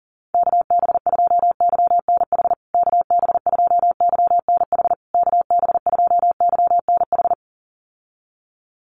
Morse code, heard as a keyed tone, about 40 wpm, 700 Hz. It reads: KB2YNH KB2YNH KB2YNH
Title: USA Call Signs Repeated 3x 40wpm